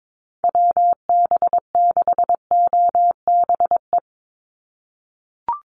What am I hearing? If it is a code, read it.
Morse code, heard as a keyed tone, about 22 wpm, 700 Hz. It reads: WB6OBE
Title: USA Call Signs 22wpm